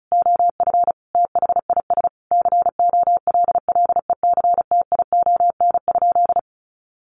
Morse code, heard as a keyed tone, about 34 wpm, 700 Hz. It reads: OF THIS COLLECTION?
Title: Sentences from Top 1200 Words 35wpm